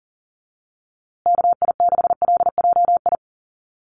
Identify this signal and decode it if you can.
Morse code, heard as a keyed tone, about 40 wpm, 700 Hz. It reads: KI6LJI
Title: USA Call Signs Repeated 2x 40wpm